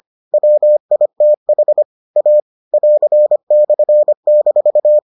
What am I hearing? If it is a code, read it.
Morse code, heard as a keyed tone, about 25 wpm, 600 Hz. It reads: WITH A <AR>/-